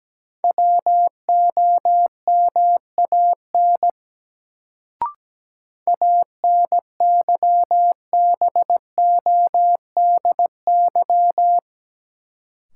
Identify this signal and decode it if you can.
Morse code, heard as a keyed tone, about 17 wpm, 700 Hz. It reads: WOMAN ANYBODY
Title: Sets of 5 Words - New Word Drill - 17wpm